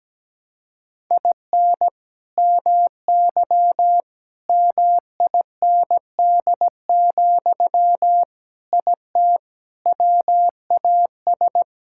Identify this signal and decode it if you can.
Morse code, heard as a keyed tone, about 17 wpm, 700 Hz. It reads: IN MY MIND, IT WAS
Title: Sentences from Top 1000 Words - Review Part 6 - 17wpm